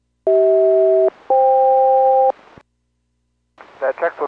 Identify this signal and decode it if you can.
Aircraft SELCAL, heard as two pairs of tones, with speech. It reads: CH-FK